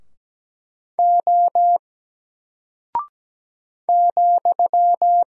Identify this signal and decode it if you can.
Morse code, heard as a keyed tone, about 17 wpm, 700 Hz. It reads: O ,
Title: Single Character (Letter:Number:Punctuation:Prosign) 17wpm